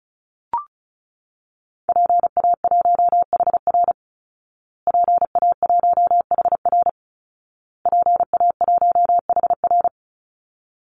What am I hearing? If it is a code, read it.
Morse code, heard as a keyed tone, about 34 wpm, 700 Hz. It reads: PA1HR PA1HR PA1HR
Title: Call Signs Repeated 2x - 35wpm